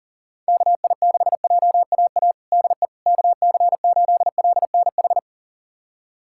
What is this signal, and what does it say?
Morse code, heard as a keyed tone, about 40 wpm, 700 Hz. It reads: KI6JAA DE KC8LNH
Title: USA Callsign DE Callsign 40wpm